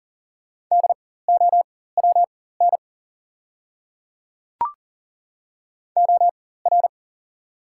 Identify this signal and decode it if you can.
Morse code, heard as a keyed tone, about 40 wpm, spaced as for 19 wpm, 690 Hz. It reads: DOWN OR
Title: 4x Character Spacing - Top 100 Words - 40wpm